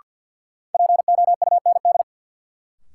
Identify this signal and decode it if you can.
Morse code, heard as a keyed tone, about 53 wpm, 700 Hz. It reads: POUND
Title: Sentences from Top 1000 Words - Review Part 2 - New Word Drill - 50wpm